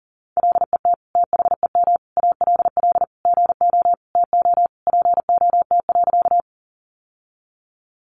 Morse code, heard as a keyed tone, about 40 wpm, 700 Hz. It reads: LET THEM ALL GO TO POT.
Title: Sentences from Top 1000 Words 40wpm